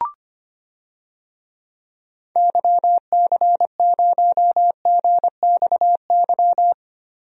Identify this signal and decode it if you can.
Morse code, heard as a keyed tone, about 25 wpm, 700 Hz. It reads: YC0GXY
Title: DXCC-100 Call Signs - 6 Characters - 25wpm